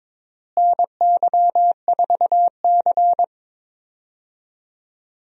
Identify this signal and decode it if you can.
Morse code, heard as a keyed tone, about 22 wpm, 700 Hz. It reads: NY4C
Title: USA Call Signs - 4 Characters - Repeated 3x - 22wpm